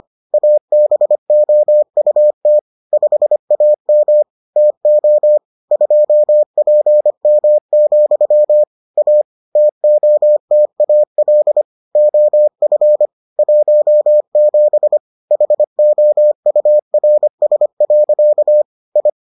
Morse code, heard as a keyed tone, about 25 wpm, 600 Hz. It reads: ABOUT 5AM TO 2PM, A TOTAL OF 17 HOURS. I